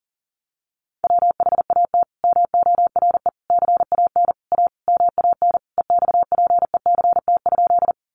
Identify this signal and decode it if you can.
Morse code, heard as a keyed tone, about 40 wpm, 700 Hz. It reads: WHAT MORE CAN A MAN EXPECT?
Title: Sentences from Top 1100 Words 40wpm